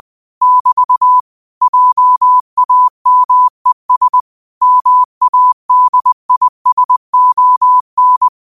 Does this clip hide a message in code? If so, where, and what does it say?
Yes: Morse code from 0.4 to 8.3 s, heard as a keyed tone, about 20 wpm, 1000 Hz. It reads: <BT> JAMES MADISON